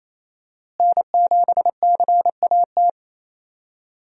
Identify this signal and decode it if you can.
Morse code, heard as a keyed tone, about 28 wpm, 700 Hz. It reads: N7CAT
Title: USA Call Signs - 5 Characters - Repeated 3x - 28wpm